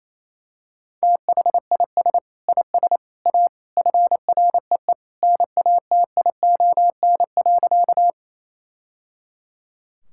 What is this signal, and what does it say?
Morse code, heard as a keyed tone, about 27 wpm, 700 Hz. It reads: THIS IS A FREE NATION.
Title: Sentences from Top 1000 Words - Review Part 6 - 28wpm